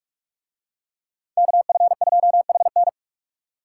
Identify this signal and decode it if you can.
Morse code, heard as a keyed tone, about 44 wpm, 700 Hz. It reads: KF2HN